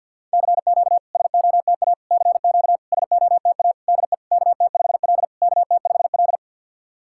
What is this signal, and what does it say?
Morse code, heard as a keyed tone, about 52 wpm, 690 Hz. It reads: CQ SOTA CQ SOTA DE KT5L KT5L
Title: SOTA QSO Practice 50wpm